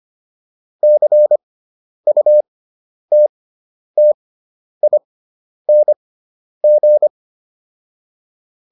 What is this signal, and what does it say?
Morse code, heard as a keyed tone, about 25 wpm, spaced as for 10 wpm, 600 Hz. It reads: CUTTING